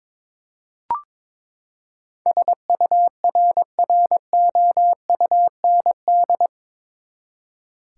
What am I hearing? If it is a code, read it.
Morse code, heard as a keyed tone, about 22 wpm, 700 Hz. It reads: SURROUND
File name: practice / 22wpm / Eight Letter Words - 22wpm